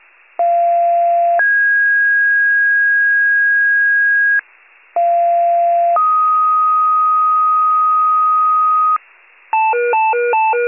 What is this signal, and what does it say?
Two-tone page, heard as a one-second tone, then a three-second tone, about 690 then 1650 Hz, about 690 then 1250 Hz, with noise. It is heard from 0.4 to 9.0 s.
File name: wcalert